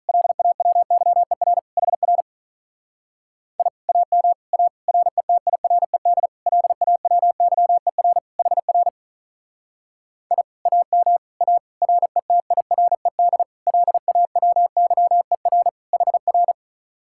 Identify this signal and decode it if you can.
Morse code, heard as a keyed tone, about 43 wpm, 700 Hz. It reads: LAWYER HR I AM A RETIRED LAWYER HR I AM A RETIRED LAWYER HR
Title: Speed-Racing Common QSO Phrases - 35wpm